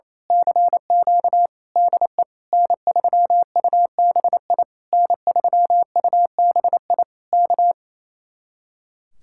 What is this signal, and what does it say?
Morse code, heard as a keyed tone, about 26 wpm, 700 Hz. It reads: CQ DE N3UBI N3UBI K
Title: Speed-Racing Common QSO Phrases - 28wpm